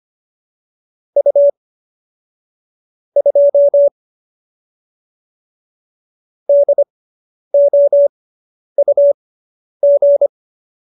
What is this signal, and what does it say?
Morse code, heard as a keyed tone, about 25 wpm, spaced as for 10 wpm, 570 Hz. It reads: U 2 DOUG
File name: QSO220 10